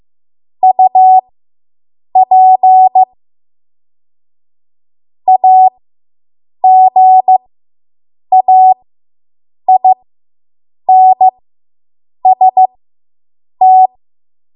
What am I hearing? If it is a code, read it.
Morse code, heard as a keyed tone, about 15 wpm, spaced as for 7 wpm, 750 Hz. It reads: UP AGAINST